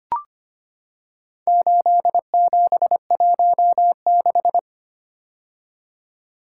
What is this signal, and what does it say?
Morse code, heard as a keyed tone, about 25 wpm, 700 Hz. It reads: 8716
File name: practice / 25wpm / Four Letter-Number Combinations - 25wpm